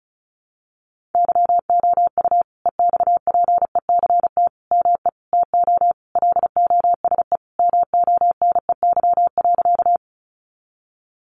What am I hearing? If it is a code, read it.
Morse code, heard as a keyed tone, about 34 wpm, 700 Hz. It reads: YOU EXPECT ME TO LOSE MONEY.
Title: Sentences from Top 1100 Words 35wpm